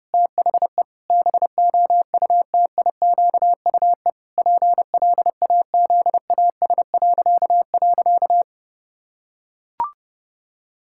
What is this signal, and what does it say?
Morse code, heard as a keyed tone, about 30 wpm, 700 Hz. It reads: THE BOUTIQUE PLAZAS..
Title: Pangram Challenge v2 - 30wpm